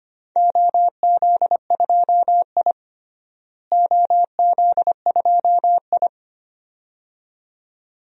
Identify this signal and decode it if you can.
Morse code, heard as a keyed tone, about 25 wpm, 700 Hz. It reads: OZ2I OZ2I
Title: DXCC-100 Call Signs - 4 Characters - Repeated 2x - 25wpm